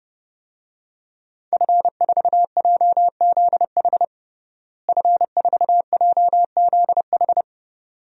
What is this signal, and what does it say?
Morse code, heard as a keyed tone, about 30 wpm, 700 Hz. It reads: F4JZH F4JZH
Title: DXCC-100 Call Signs - 5 Characters - Repeated 2x - 30wpm